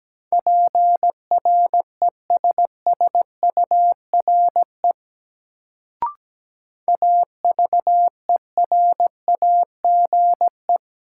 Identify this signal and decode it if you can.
Morse code, heard as a keyed tone, about 17 wpm, 700 Hz. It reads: PRESSURE AVERAGE
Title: Sentences from Top 1400 Words - New Word Drill Alt - 17wpm